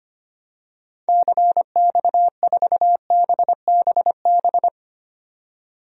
Morse code, heard as a keyed tone, about 25 wpm, 700 Hz. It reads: CX4BBB
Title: DXCC-100 Call Signs - 6 Characters - 25wpm